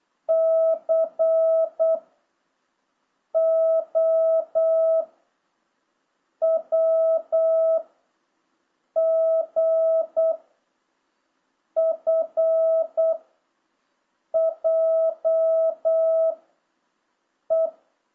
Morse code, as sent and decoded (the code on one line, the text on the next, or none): -.-. / --- / .-- / --. / ..-. / .--- / .
C O W G F J E